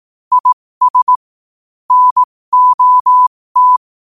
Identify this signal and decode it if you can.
Morse code, heard as a keyed tone, about 18 wpm, spaced as for 15 wpm, 1000 Hz. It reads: IS NOT
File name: qotdfiles / 15wpm